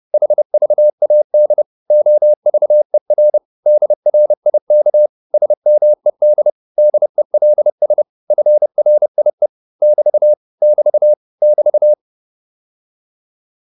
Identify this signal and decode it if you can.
Morse code, heard as a keyed tone, about 30 wpm, 600 Hz. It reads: HVAD OVER DRIK SMED DELS FRIE <BT> <BT> <BT>
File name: Korte ord DK 30wpm